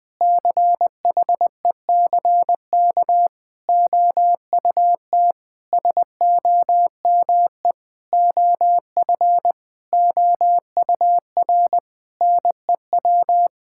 Morse code, heard as a keyed tone, about 20 wpm, 700 Hz. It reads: CHECK OUT SOME OF OUR NEW
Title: Sentences with 9-Word Phrases - 20wpm